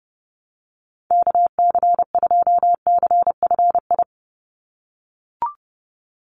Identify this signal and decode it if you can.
Morse code, heard as a keyed tone, about 30 wpm, 700 Hz. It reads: KC2CFI
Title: USA Call Signs 30wpm